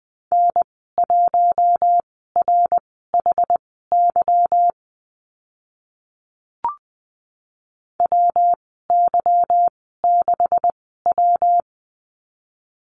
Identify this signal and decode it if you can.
Morse code, heard as a keyed tone, about 20 wpm, spaced as for 15 wpm, 700 Hz. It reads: N1RHY WY6W
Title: 167 Lesson 27g - Number 7 - Call Signs 20wpm